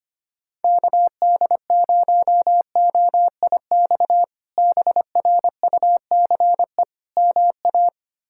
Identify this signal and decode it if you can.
Morse code, heard as a keyed tone, about 25 wpm, 700 Hz. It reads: KD0OIX BRUCE MA
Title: K1USN SST QSO Practice - 25wpm